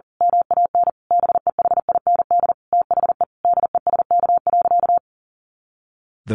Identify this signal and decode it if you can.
Morse code, heard as a keyed tone, about 40 wpm, 700 Hz. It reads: MAN BEHIND THE DESK.
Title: A Sound Of Thunder 40wpm